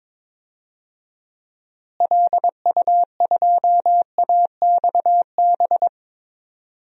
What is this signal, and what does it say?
Morse code, heard as a keyed tone, about 22 wpm, 700 Hz. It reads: LU2AXB
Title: DXCC-100 Call Signs - 6 Characters - Repeated 3x - 22wpm